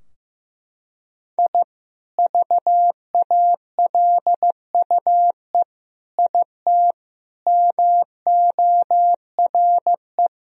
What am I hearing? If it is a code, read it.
Morse code, heard as a keyed tone, about 15 wpm, 700 Hz. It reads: I VALUE IT MORE
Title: Sentences from Top 1000 Words - Review Part 3 - 15wpm